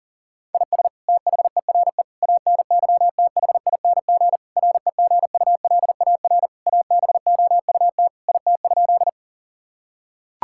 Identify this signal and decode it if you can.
Morse code, heard as a keyed tone, about 40 wpm, 700 Hz. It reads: IS THERE ANYTHING REGULAR ABOUT IT?
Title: Sentences from Top 1000 Words - Review Part 6 - 40wpm